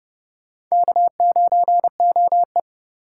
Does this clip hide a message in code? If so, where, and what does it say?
Yes: Morse code from 0.7 to 2.6 s, heard as a keyed tone, about 26 wpm, 700 Hz. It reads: K9OE